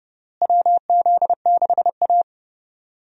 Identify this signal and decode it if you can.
Morse code, heard as a keyed tone, about 32 wpm, 700 Hz. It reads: WZ6A
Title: USA Call Signs - 4 Characters - Repeated 2x - 30wpm